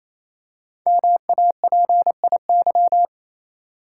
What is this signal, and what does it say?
Morse code, heard as a keyed tone, about 28 wpm, 700 Hz. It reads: MAPIY
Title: Code Groups - 5 Letters - 28wpm